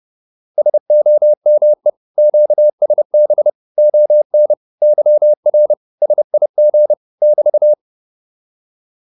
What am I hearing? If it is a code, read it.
Morse code, heard as a keyed tone, about 30 wpm, 600 Hz. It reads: SOME QSB ON YR SIG <BT>